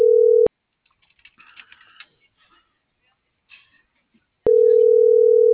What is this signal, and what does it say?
Ringback tone (North America), 440 + 480 Hz.